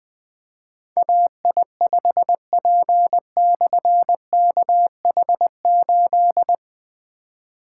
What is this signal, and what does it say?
Morse code, heard as a keyed tone, about 20 wpm, 700 Hz. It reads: AI5P/KH8
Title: Difficult Call Signs 20wpm